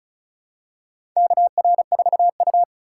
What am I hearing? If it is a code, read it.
Morse code, heard as a keyed tone, about 34 wpm, 700 Hz. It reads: KR4U
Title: USA Call Signs - 4 Characters - 2x Character Spacing - 35wpm